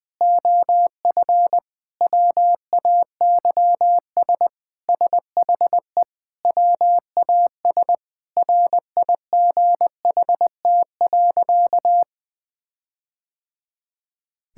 Morse code, heard as a keyed tone, about 20 wpm, 700 Hz. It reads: OF WAYS SHE WAS RIGHT.
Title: Sentences with 4-Word Phrases - Spoken Beginning - 20wpm